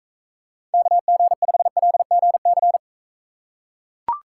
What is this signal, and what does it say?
Morse code, heard as a keyed tone, about 39 wpm, 700 Hz. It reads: KG5LGC